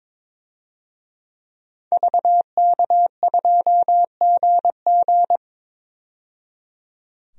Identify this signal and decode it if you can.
Morse code, heard as a keyed tone, about 22 wpm, 700 Hz. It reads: VK2GG
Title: DXCC-100 Call Signs - 5 Characters - 22wpm